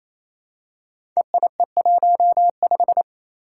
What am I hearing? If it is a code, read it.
Morse code, heard as a keyed tone, about 27 wpm, 700 Hz. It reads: EIE15
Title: Five Challenging Letter-Number Combinations 28wpm